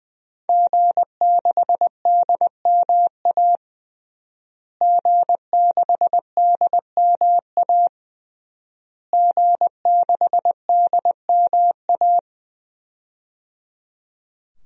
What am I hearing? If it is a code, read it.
Morse code, heard as a keyed tone, about 20 wpm, 700 Hz. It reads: G6DMA G6DMA G6DMA
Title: DXCC-100 Call Signs - 5 Characters - Repeated 3x - 20wpm